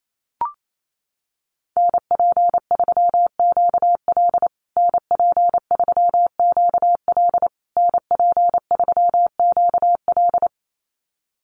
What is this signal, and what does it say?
Morse code, heard as a keyed tone, about 28 wpm, 700 Hz. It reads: NP3QL NP3QL NP3QL
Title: USA Call Signs Repeated 3x 28wpm